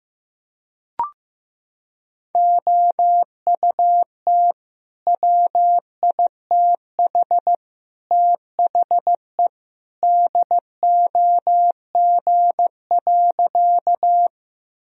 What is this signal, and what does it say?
Morse code, heard as a keyed tone, about 15 wpm, 700 Hz. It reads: OUT WITH THE DOG.
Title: Sentences from Top 1000 Words - Review Part 2 - 15wpm